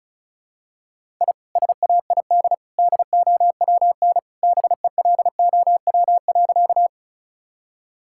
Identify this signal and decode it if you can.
Morse code, heard as a keyed tone, about 36 wpm, 700 Hz. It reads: I SAID DOWN BELOW.
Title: Sentences from Top 1000 Words - Review Part 5 - 35wpm